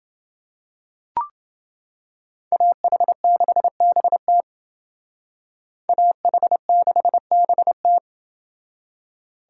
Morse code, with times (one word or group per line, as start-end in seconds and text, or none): 2.5-4.4 AH6BT
5.9-8.0 AH6BT